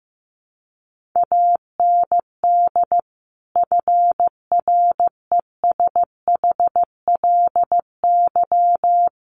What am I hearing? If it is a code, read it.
Morse code, heard as a keyed tone, about 15 wpm, 700 Hz. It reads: AND FRESHLY
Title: Sets of 5 Words Encore 15wpm